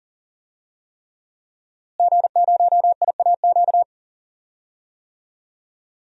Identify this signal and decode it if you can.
Morse code, heard as a keyed tone, about 40 wpm, 700 Hz. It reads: G0IAQ
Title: DXCC-100 Call Signs - 5 Characters - Repeated 3x - 40wpm